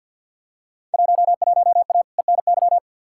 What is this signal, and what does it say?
Morse code, heard as a keyed tone, about 51 wpm, 700 Hz. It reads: 11A ENY
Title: Field Day Exchanges Encore - 50wpm